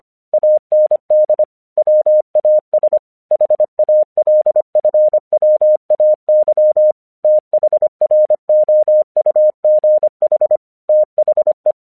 Morse code, heard as a keyed tone, about 25 wpm, 610 Hz. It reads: AND WAS HALFWAY THROUGH THE
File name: 1 English 25wpm / AQC23